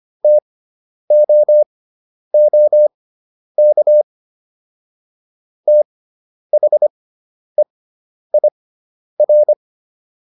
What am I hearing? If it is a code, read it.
Morse code, heard as a keyed tone, about 25 wpm, spaced as for 10 wpm, 600 Hz. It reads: TOOK THEIR